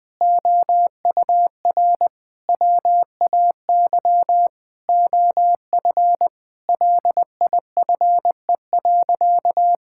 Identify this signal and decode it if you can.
Morse code, heard as a keyed tone, about 20 wpm, 700 Hz. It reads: OUR WAY OF LIFE.
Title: Sentences from Top 1000 Words - Review Part 6 - 20wpm